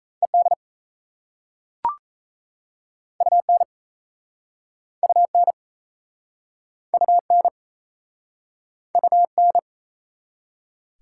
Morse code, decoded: ED UN UN UN UN

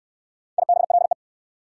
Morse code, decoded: IBLE

